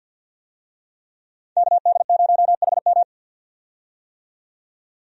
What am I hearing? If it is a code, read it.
Morse code, heard as a keyed tone, about 52 wpm, 700 Hz. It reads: KD0HM
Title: USA Call Signs - 5 Characters - 50wpm